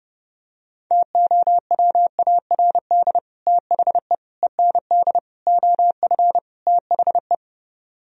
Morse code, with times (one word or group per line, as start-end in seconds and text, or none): 0.9-3.2 TOWARD
3.5-4.1 THE
4.4-5.2 END
5.5-6.4 OF
6.7-7.3 THE